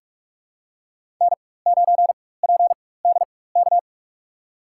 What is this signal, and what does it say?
Morse code, heard as a keyed tone, about 46 wpm, spaced as for 21 wpm, 700 Hz. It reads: N9PDK
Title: USA Call Signs - 5 Characters - 4x Character Spacing - 45wpm